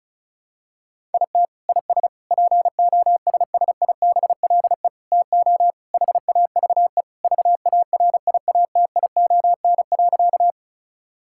Morse code, heard as a keyed tone, about 34 wpm, 700 Hz. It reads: IT IS POSSIBLE TO HAVE VARIATION.
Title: Sentences from Top 1600 Words 35wpm